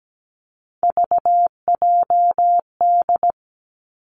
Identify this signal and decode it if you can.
Morse code, heard as a keyed tone, about 17 wpm, 700 Hz. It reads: VJD
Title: Three Challenging Letter-Number Combinations 17wpm